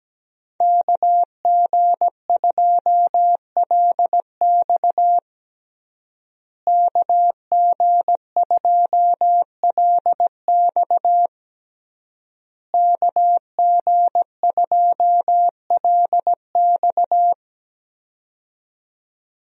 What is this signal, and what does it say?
Morse code, heard as a keyed tone, about 17 wpm, 700 Hz. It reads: KG2LX KG2LX KG2LX
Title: USA Call Signs - 5 Characters - Repeated 3x - 17wpm